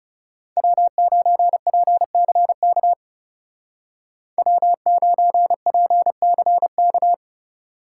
Morse code, heard as a keyed tone, about 36 wpm, 700 Hz. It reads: W9PCK W9PCK
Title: Speed Racing - USA 1x3 Call Signs 30wpm